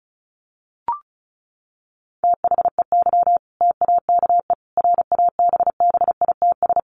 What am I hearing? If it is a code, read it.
Morse code, heard as a keyed tone, about 34 wpm, 700 Hz. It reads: THEY TAKE RABBITS
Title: Easy Phrases 35wpm